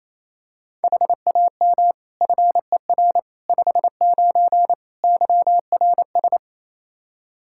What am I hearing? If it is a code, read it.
Morse code, heard as a keyed tone, about 28 wpm, 700 Hz. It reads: HAM FER 59 YRS
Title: Practice QSO - Exchange 3 - More Info - 28wpm